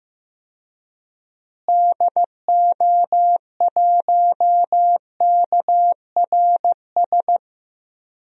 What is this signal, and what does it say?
Morse code, heard as a keyed tone, about 16 wpm, 700 Hz. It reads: DO1KRS
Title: DXCC-100 Call Signs - 6 Characters - Repeated 3x - 15wpm